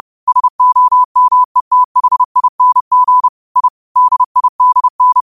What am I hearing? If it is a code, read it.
Morse code, heard as a keyed tone, about 30 wpm, 1000 Hz. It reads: SOMETHING I DIDN
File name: qotdfiles / 30wpm